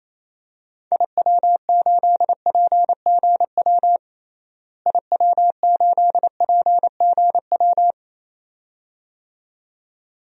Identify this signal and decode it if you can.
Morse code, heard as a keyed tone, about 28 wpm, 700 Hz. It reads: IW8PGW IW8PGW
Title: DXCC-100 Call Signs - 6 Characters - Repeated 2x - 28wpm